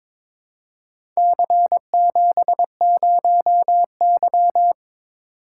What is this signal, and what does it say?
Morse code, heard as a keyed tone, about 23 wpm, 700 Hz. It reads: C70Y
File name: practice / 22wpm / Four Letter-Number Combinations - 22wpm